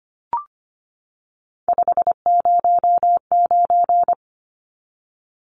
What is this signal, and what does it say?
Morse code, heard as a keyed tone, about 25 wpm, 700 Hz. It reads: 509
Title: Numbers - 3-Digit 25wpm